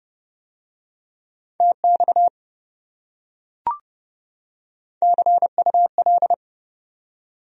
Morse code, heard as a keyed tone, about 28 wpm, 700 Hz. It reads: TX CUL